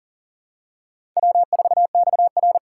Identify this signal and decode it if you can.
Morse code, heard as a keyed tone, about 37 wpm, 700 Hz. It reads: W4XR
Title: USA Call Signs - 4 Characters - 3x Character Spacing - 40wpm